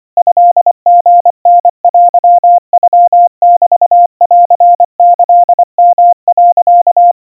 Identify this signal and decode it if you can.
Morse code (Japanese Wabun code), heard as a keyed tone, about 24 wpm, 700 Hz. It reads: トリタテノメンキヨ、
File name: kawa / Morse sample / kamyu 90 cpm 700 Hz Crispy